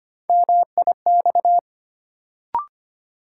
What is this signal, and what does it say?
Morse code, heard as a keyed tone, about 25 wpm, 700 Hz. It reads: MIX
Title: Sentences from Top 1100 Words - New Word Drill Alt - 25wpm